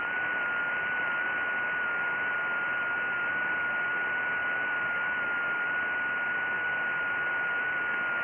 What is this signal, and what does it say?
FT8.
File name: IC-7100